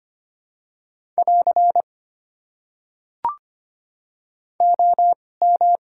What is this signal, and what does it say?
Morse code, heard as a keyed tone, about 25 wpm, spaced as for 18 wpm, 700 Hz. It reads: <AR> OM
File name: Top 90 QSO Elements - 2x Character Spacing - 25wpm